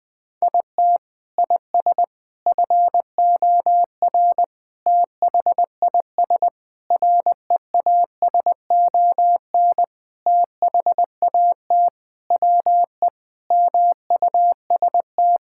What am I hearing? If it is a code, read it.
Morse code, heard as a keyed tone, about 20 wpm, 700 Hz. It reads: IT IS FOR THIS REASON THAT WE MUST
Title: Sentences with 6-Word Phrases - 20wpm